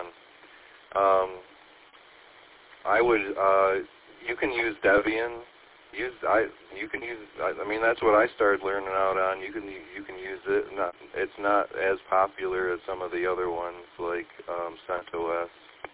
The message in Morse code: none

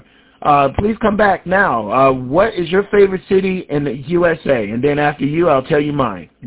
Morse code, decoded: none

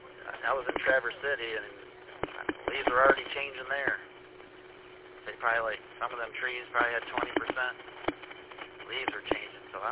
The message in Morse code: none

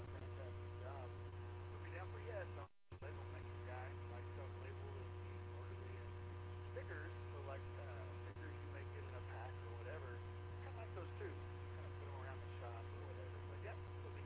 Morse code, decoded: none

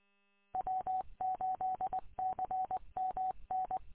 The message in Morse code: W8CMN

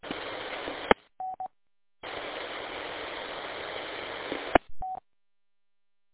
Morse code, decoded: none

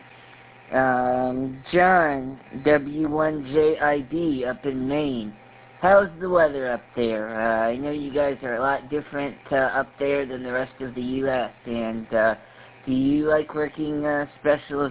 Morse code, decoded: none

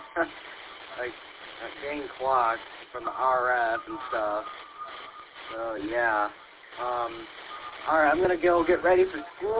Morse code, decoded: none

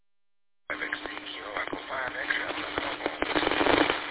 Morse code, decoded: none

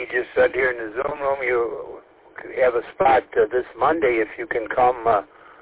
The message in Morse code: none

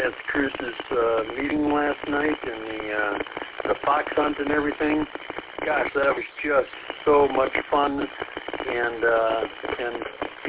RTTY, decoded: none

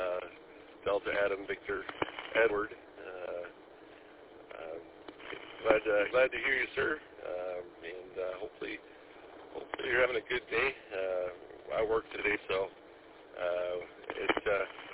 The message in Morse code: none

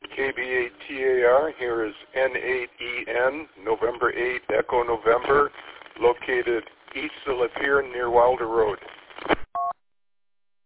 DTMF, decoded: none

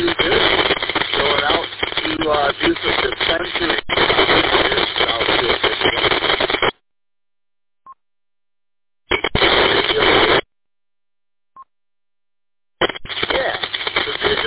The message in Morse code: none

